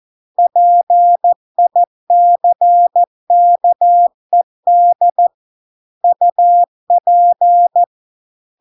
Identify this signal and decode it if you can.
Morse code, heard as a keyed tone, about 14 wpm, 700 Hz. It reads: PICKED UP